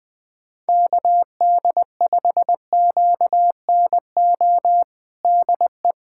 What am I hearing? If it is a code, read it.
Morse code, heard as a keyed tone, about 20 wpm, 700 Hz. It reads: KD5QNO DE
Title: USA Callsign DE Callsign 20wpm